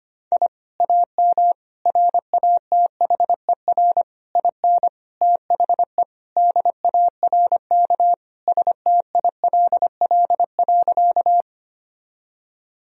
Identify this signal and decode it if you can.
Morse code, heard as a keyed tone, about 25 wpm, 700 Hz. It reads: I AM RATHER IN THE DARK STILL.
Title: Sentences from Top 1000 Words - Review Part 4 - 25wpm